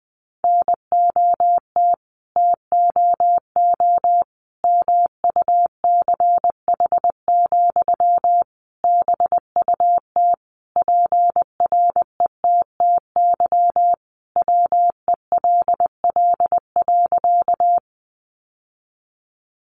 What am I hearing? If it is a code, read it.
Morse code, heard as a keyed tone, about 20 wpm, 700 Hz. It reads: NOT TOO MUCH, BUT PRETTY WELL.
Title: Sentences from Top 1000 Words 20wpm